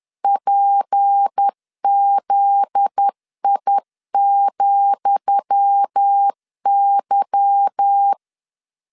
Morse code, heard as a keyed tone, about 10 wpm, 800 Hz. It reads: PZI,Y
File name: LEZIONI / 10 SIGNS / 10D COMA 10wpm 3013